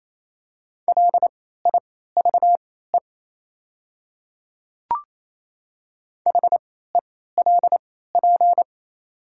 Morse code, heard as a keyed tone, about 28 wpm, spaced as for 16 wpm, 700 Hz. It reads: LIVE HELP
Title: 3x Character Spacing - Top 200 Words - 28wpm